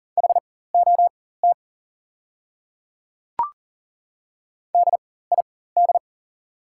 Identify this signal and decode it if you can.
Morse code, heard as a keyed tone, about 40 wpm, spaced as for 19 wpm, 700 Hz. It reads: HOT DID
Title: 4x Character Spacing - Top 200 Words - 40wpm